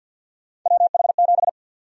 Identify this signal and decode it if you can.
Morse code, heard as a keyed tone, about 50 wpm, 700 Hz. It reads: WH7
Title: USA Call Sign Prefixes - 50wpm